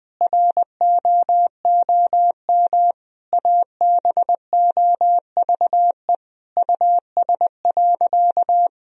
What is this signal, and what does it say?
Morse code, heard as a keyed tone, about 20 wpm, 700 Hz. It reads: ROOM ABOVE US.
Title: Sentences from Top 1000 Words - Review Part 3 - 20wpm